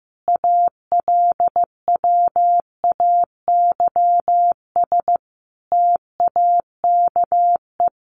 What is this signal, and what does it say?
Morse code, heard as a keyed tone, about 15 wpm, 700 Hz. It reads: ALWAYS TAKE
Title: Sentences from Top 900 Words 15wpm